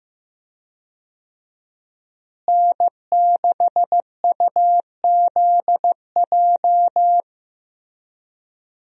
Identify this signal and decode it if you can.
Morse code, heard as a keyed tone, about 15 wpm, 700 Hz. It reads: N6UZJ
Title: USA Call Signs - 5 Characters - Repeated 3x - 15wpm